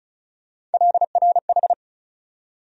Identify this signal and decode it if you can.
Morse code, heard as a keyed tone, about 34 wpm, 700 Hz. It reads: LRH
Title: USA Call Sign Postfixes - 3 Letters 35wpm